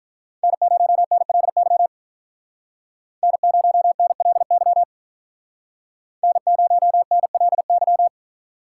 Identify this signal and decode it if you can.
Morse code, heard as a keyed tone, about 45 wpm, 700 Hz. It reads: N0NLY N0NLY N0NLY